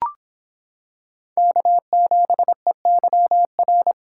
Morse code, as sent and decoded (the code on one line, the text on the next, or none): -.- --... -.-- .-.
K7EYR